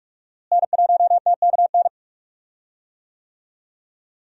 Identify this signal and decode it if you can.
Morse code, heard as a keyed tone, about 46 wpm, 700 Hz. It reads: N1TKN
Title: USA Call Signs - 5 Characters - Repeated 2x - 45wpm